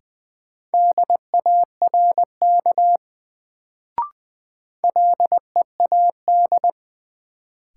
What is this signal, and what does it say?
Morse code, heard as a keyed tone, about 20 wpm, 700 Hz. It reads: DARK LEAD
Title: Sentences from Top 500 Words - New Word Drill - 20wpm